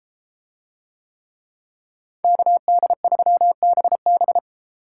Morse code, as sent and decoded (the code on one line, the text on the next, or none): -.- -.. ...-- -... -...
KD3BB